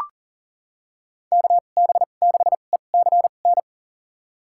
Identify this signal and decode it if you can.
Morse code, heard as a keyed tone, about 40 wpm, spaced as for 29 wpm, 700 Hz. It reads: KB6ECN
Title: USA Call Signs - 6 Characters - 2x Character Spacing - 40wpm